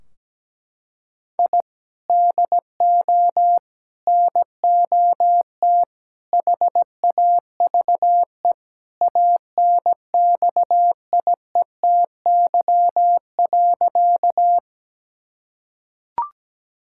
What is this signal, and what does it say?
Morse code, heard as a keyed tone, about 17 wpm, 700 Hz. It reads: I DO NOT HAVE ANXIETY.
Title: Sentences from Top 1600 Words 17wpm